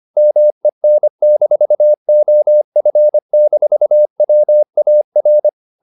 Morse code, heard as a keyed tone, about 25 wpm, 600 Hz. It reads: MEN-OF-WAR
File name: the old man and the sea part 1 25wpm